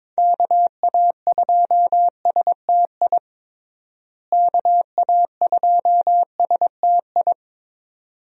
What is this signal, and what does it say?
Morse code, heard as a keyed tone, about 22 wpm, 700 Hz. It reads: KA2STI KA2STI
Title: USA Call Signs - 6 Characters - Repeated 2x - 22wpm